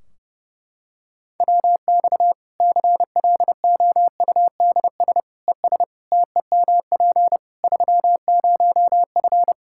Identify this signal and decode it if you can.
Morse code, heard as a keyed tone, about 32 wpm, 700 Hz. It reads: WX CLOUDS ES TEMP 30F
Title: K1USN SST QSO Practice 30wpm